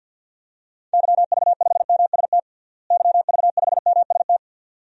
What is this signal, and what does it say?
Morse code, heard as a keyed tone, about 48 wpm, 700 Hz. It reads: YV5MST YV5MST
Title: DXCC-100 Call Signs - 6 Characters - Repeated 2x - 50wpm